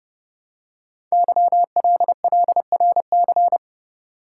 Morse code, heard as a keyed tone, about 30 wpm, 700 Hz. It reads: YLLRC
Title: Code Groups - 5 Letters - 30wpm